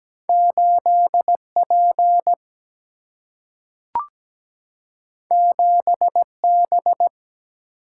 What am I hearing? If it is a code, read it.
Morse code, heard as a keyed tone, about 17 wpm, 700 Hz. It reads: 8 P 7 B
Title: Single-Letter Single-Number 17wpm